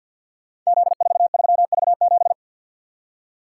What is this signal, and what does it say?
Morse code, heard as a keyed tone, about 50 wpm, 700 Hz. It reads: Z43V7
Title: Code Groups - 5 Letter:Numbers - 50wpm